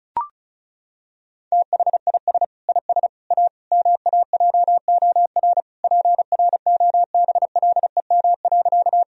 Morse code, heard as a keyed tone, about 36 wpm, 700 Hz. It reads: THIS IS A MAJOR PROBLEM.
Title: Sentences from Top 1000 Words - Review Part 4 - 35wpm